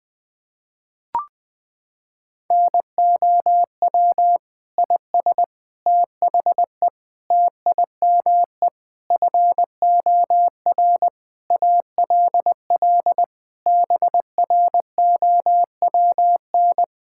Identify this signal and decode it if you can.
Morse code, heard as a keyed tone, about 19 wpm, 700 Hz. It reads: NOW IS THE TIME FOR ALL BROWN
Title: Pangram Challenge v2 - 20wpm